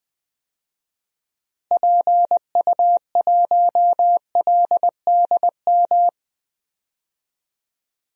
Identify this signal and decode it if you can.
Morse code, heard as a keyed tone, about 20 wpm, 700 Hz. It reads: PU1LDM
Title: DXCC-100 Call Signs - 6 Characters - Repeated 3x - 20wpm